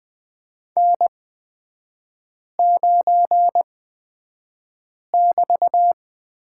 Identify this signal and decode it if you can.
Morse code, heard as a keyed tone, about 20 wpm, spaced as for 5 wpm, 700 Hz. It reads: N9<BT>
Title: 224 Lesson 36e - BT - Three Characters 20wpm